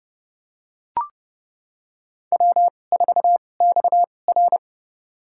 Morse code, heard as a keyed tone, about 30 wpm, spaced as for 22 wpm, 700 Hz. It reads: W4XR
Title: USA Call Signs - 2x Character Spacing - 30wpm